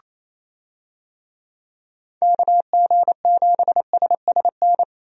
Morse code, heard as a keyed tone, about 27 wpm, 700 Hz. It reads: KG7SSN
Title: USA Call Signs - 6 Characters - 28wpm